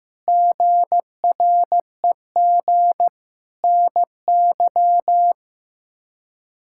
Morse code, heard as a keyed tone, about 15 wpm, 700 Hz. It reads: GREG NY
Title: K1USN SST QSO Practice - 15wpm